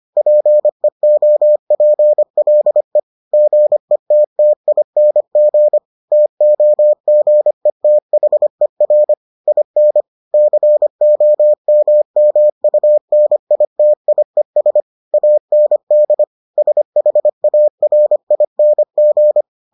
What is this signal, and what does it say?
Morse code, heard as a keyed tone, about 25 wpm, 600 Hz. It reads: PEOPLE GETTING TOGETHER IN COMMUNITIES AND SHARING